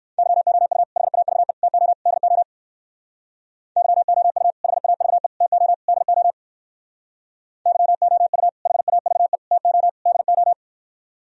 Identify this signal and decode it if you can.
Morse code, heard as a keyed tone, about 57 wpm, 700 Hz. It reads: YOU HAVE TO DO YOU HAVE TO DO YOU HAVE TO DO